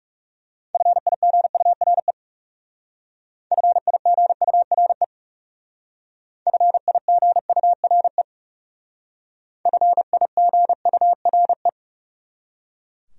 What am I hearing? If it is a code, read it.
Morse code, heard as a keyed tone, about 39 wpm, 700 Hz. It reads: FIGURE FIGURE FIGURE FIGURE